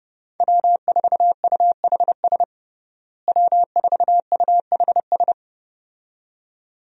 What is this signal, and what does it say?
Morse code, heard as a keyed tone, about 30 wpm, 700 Hz. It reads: W4UHS W4UHS
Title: USA Call Signs - 5 Characters - Repeated 2x - 30wpm